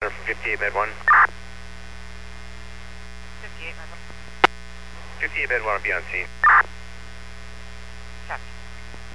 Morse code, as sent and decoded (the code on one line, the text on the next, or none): none